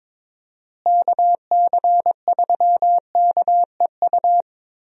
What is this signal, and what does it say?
Morse code, heard as a keyed tone, about 22 wpm, 700 Hz. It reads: KC3KEU
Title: USA Call Signs - 6 Characters - Repeated 3x - 22wpm